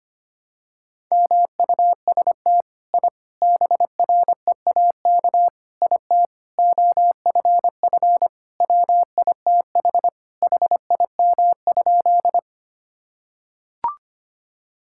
Morse code, heard as a keyed tone, about 25 wpm, 700 Hz. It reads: MUST I BREAK IT OFF WITH HIM?
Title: Sentences from Top 1000 Words - Review Part 6 - 25wpm